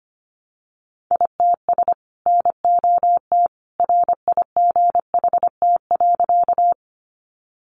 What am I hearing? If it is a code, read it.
Morse code, heard as a keyed tone, about 25 wpm, 700 Hz. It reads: ITS NOT RIGHT.
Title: Sentences from Top 600 Words 25wpm